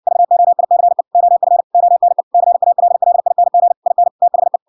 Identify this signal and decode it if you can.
Morse code, heard as a keyed tone, about 58 wpm, 700 Hz. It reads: VOICE OF ONE CALLING IN THE